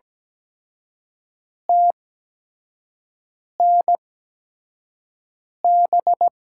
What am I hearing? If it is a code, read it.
Morse code, heard as a keyed tone, about 17 wpm, spaced as for 5 wpm, 700 Hz. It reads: TNB
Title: USA Call Signs - 8x Character Spacing - 17wpm